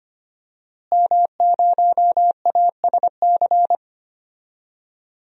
Morse code, heard as a keyed tone, about 25 wpm, 700 Hz. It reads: M0ASC
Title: DXCC-100 Call Signs - 5 Characters - Repeated 2x - 25wpm